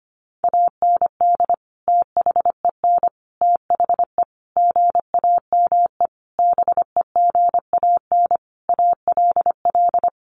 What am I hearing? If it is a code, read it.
Morse code, heard as a keyed tone, about 25 wpm, 700 Hz. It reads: AND THEN THE GAME BEGAN ALL
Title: Sentences from Top 500 Words 25wpm